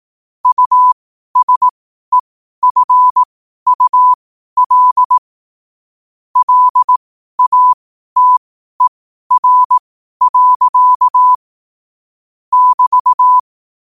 Morse code, heard as a keyed tone, about 18 wpm, spaced as for 12 wpm, 1000 Hz. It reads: USEFUL LATER. <BT>
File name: qotdfiles / 12wpm